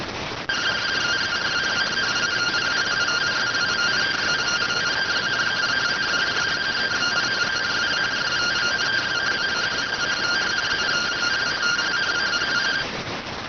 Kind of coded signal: two-tone teleprinter signal

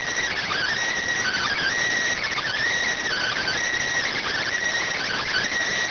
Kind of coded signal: Emergency Alert System (SAME) tones

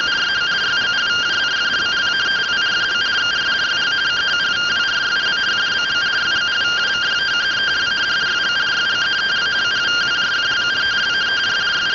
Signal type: two-tone teleprinter signal

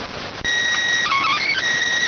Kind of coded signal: SSTV (slow-scan television)

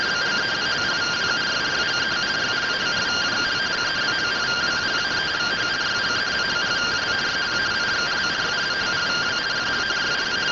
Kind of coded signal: two-tone teleprinter signal

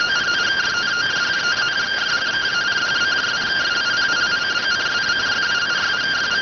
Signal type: two-tone teleprinter signal

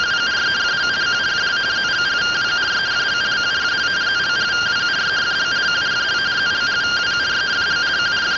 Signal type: two-tone teleprinter signal

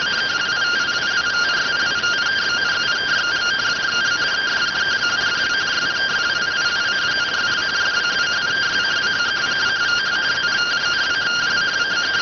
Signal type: two-tone teleprinter signal

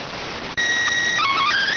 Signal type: SSTV (slow-scan television)

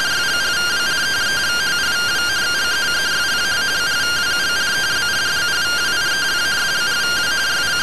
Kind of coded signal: two-tone teleprinter signal

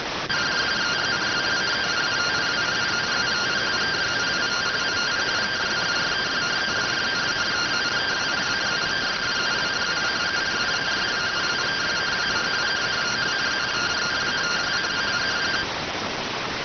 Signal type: two-tone teleprinter signal

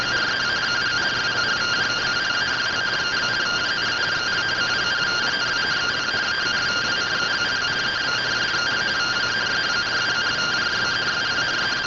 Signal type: two-tone teleprinter signal

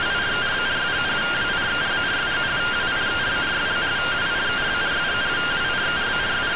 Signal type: two-tone teleprinter signal

